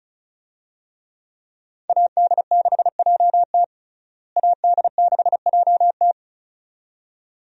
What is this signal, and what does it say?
Morse code, heard as a keyed tone, about 36 wpm, 700 Hz. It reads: AD6JT AD6JT
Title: USA Call Signs - 5 Characters - Repeated 2x - 35wpm